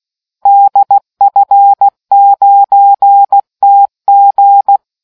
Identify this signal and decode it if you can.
Morse code, heard as a keyed tone, about 16 wpm, 800 Hz. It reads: DF9TG